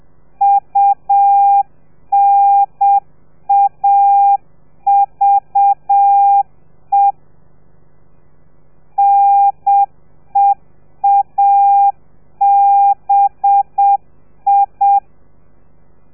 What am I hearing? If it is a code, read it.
Morse code, heard as a keyed tone, about 7 wpm, 800 Hz. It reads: UNAVE NEABI
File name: telegrafia / i2tfj / Lez4 30